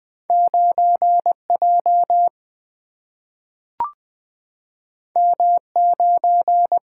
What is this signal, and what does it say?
Morse code, heard as a keyed tone, about 20 wpm, 700 Hz. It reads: 9 J M 9
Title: Single-Letter Single-Number 20wpm